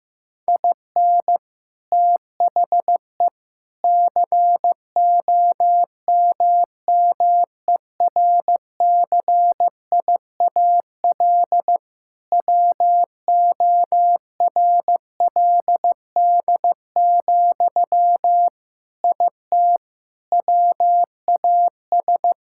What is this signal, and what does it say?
Morse code, heard as a keyed tone, about 16 wpm, 700 Hz. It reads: IN THE COMMERCIAL WORLD, IT WAS
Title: Sentences from Top 1000 Words - Review Part 5 - 15wpm